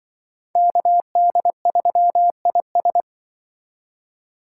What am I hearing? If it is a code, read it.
Morse code, heard as a keyed tone, about 24 wpm, 700 Hz. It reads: KD3IS
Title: Speed Racing - USA 2x2 Call Signs 20wpm